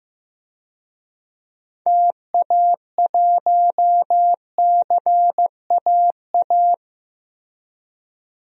Morse code, heard as a keyed tone, about 15 wpm, 700 Hz. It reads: TA1CAA
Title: DXCC-100 Call Signs - 6 Characters - Repeated 2x - 15wpm